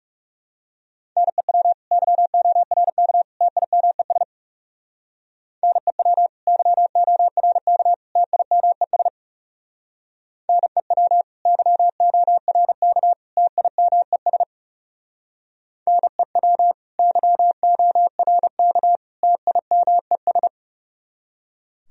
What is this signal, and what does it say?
Morse code, heard as a keyed tone, about 39 wpm, 700 Hz. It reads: NEW YORK TIMES NEW YORK TIMES NEW YORK TIMES NEW YORK TIMES